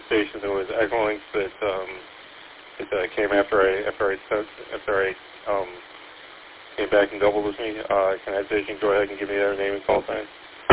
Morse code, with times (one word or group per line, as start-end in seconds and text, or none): none